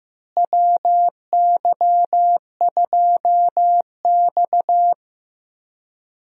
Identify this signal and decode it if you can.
Morse code, heard as a keyed tone, about 15 wpm, 700 Hz. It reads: WY2X
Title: USA Call Signs - 4 Characters - Repeated 2x - 15wpm